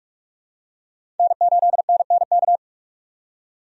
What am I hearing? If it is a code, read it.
Morse code, heard as a keyed tone, about 42 wpm, 700 Hz. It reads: N8NNK
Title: USA Call Signs - 5 Characters - 45wpm